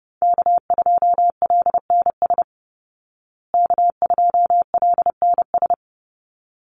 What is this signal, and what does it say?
Morse code, heard as a keyed tone, about 30 wpm, 700 Hz. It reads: K2LNS K2LNS
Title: Call Signs Repeated 3x - 30wpm